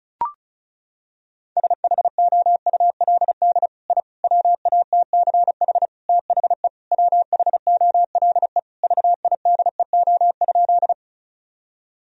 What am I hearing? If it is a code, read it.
Morse code, heard as a keyed tone, about 36 wpm, 700 Hz. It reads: SHOULD I WATCH THE WHOLE VIDEO?
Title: Sentences from Top 1000 Words - Review Part 4 - 35wpm